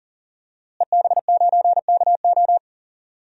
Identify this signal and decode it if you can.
Morse code, heard as a keyed tone, about 40 wpm, 700 Hz. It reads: EB9KO
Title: DXCC-100 Call Signs - 5 Characters - Repeated 3x - 40wpm